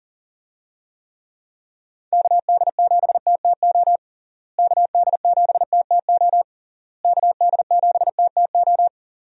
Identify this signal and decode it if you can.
Morse code, heard as a keyed tone, about 37 wpm, 700 Hz. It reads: KD7TTO KD7TTO KD7TTO
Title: USA Call Signs - 6 Characters - Repeated 3x - 40wpm